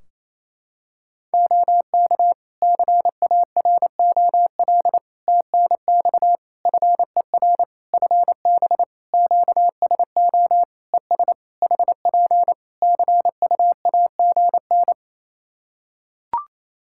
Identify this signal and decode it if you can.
Morse code, heard as a keyed tone, about 28 wpm, 700 Hz. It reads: OK CAROL TNX FER FB QSO ES HP CUAGN
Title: Practice QSO - Exchange 4 - Ending - 28wpm